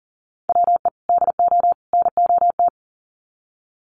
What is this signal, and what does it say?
Morse code, heard as a keyed tone, about 40 wpm, 700 Hz. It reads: WE DO NOT
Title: Sets of 3 Words 40wpm